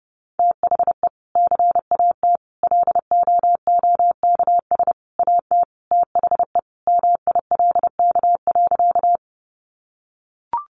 Morse code, heard as a keyed tone, about 30 wpm, 700 Hz. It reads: THE CAT LOOKS AT THE MILK.
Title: Additional Easy Phrases 30wpm